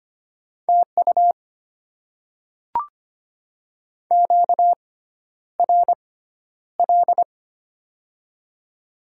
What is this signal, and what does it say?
Morse code, heard as a keyed tone, about 25 wpm, spaced as for 11 wpm, 700 Hz. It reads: TU QRL
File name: Top 20 QSO Elements - 6x Character Spacing - 25wpm